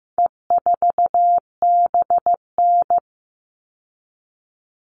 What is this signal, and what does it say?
Morse code, heard as a keyed tone, about 15 wpm, 700 Hz. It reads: E4BN
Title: All Cut Numbers - 4-Digit 15wpm